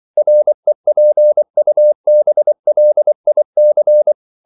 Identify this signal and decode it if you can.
Morse code, heard as a keyed tone, about 24 wpm, 600 Hz. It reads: REPUBLIC
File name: CW / Practice / NationalCapitals24